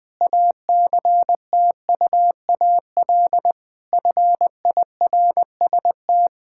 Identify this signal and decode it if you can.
Morse code, heard as a keyed tone, about 20 wpm, 700 Hz. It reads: ACTUAL FIRST
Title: Sentences from Top 1000 Words - Review Part 5 - 20wpm